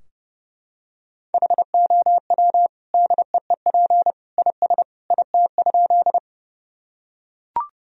Morse code, heard as a keyed tone, about 30 wpm, 700 Hz. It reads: HOW DEEP IS IT?
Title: Sentences from Top 1000 Words - Review Part 3 - 30wpm